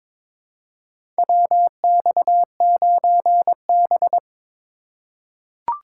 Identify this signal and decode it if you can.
Morse code, heard as a keyed tone, about 22 wpm, 700 Hz. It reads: WX9B
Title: USA Call Signs - 4 Characters - 2x Character Spacing - 22wpm